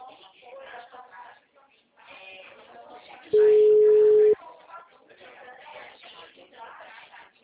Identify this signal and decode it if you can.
Ringback tone (Europe, 425 Hz), with speech.